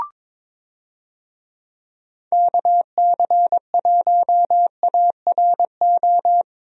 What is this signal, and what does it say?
Morse code, heard as a keyed tone, about 22 wpm, 700 Hz. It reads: KC1ARO
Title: USA Call Signs - 6 Characters - Repeated 3x - 22wpm